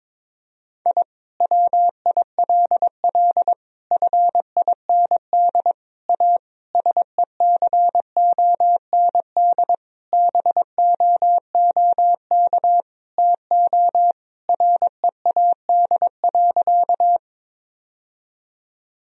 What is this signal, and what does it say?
Morse code, heard as a keyed tone, about 22 wpm, 700 Hz. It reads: I WILL FIND A SECOND BOOK TO READ.
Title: Sentences from Top 1000 Words - Review Part 1 - 22wpm